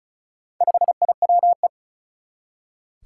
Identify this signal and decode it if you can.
Morse code, heard as a keyed tone, about 36 wpm, 700 Hz. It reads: 5IWE
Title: Four Challenging Letter-Number Combinations 35wpm